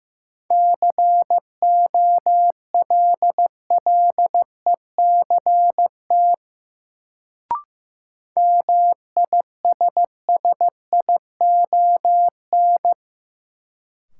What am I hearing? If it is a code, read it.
Morse code, heard as a keyed tone, about 15 wpm, 700 Hz. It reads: COLLECT MISSION
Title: Sentences from Top 1600 Words - New Word Drill - 15wpm